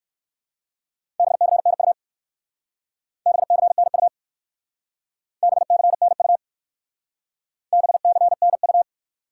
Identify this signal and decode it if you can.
Morse code, heard as a keyed tone, about 55 wpm, 700 Hz. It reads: BCNU BCNU BCNU BCNU
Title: Speed-Racing Top 90 QSO Elements 45wpm